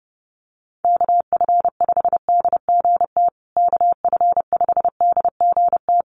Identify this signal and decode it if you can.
Morse code, heard as a keyed tone, about 30 wpm, 700 Hz. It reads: KF5DGT KF5DGT
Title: USA Call Signs Repeated 3x 30wpm